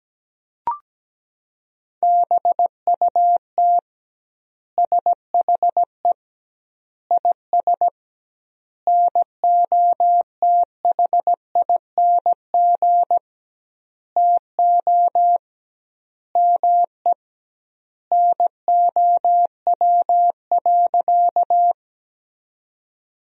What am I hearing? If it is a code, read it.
Morse code, heard as a keyed tone, about 17 wpm, 700 Hz. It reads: BUT SHE IS NOTHING TO ME NOW.
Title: 2x Word-Spacing - Sentences from Top 500 Words - 17wpm